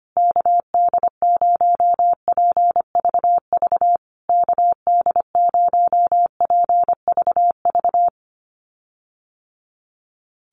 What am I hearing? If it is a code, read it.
Morse code, heard as a keyed tone, about 25 wpm, 700 Hz. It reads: KD0PVV KD0PVV
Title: USA Call Signs Repeated 3x 25wpm